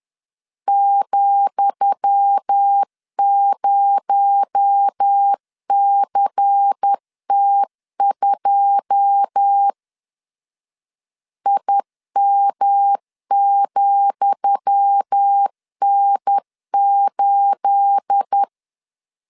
Morse code, heard as a keyed tone, about 10 wpm, 800 Hz. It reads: ,0CT2 IM,N8
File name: LEZIONI / 10 SIGNS / 10D COMA 10wpm 3013